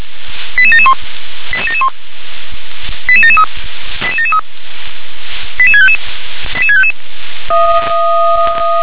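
Five-tone selective call (ZVEI), heard as quick runs of five tones, tones of about 70 ms. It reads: 88771 88771 88773 88773 88655 88655
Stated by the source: ZVEI five-tone call received with an SDR.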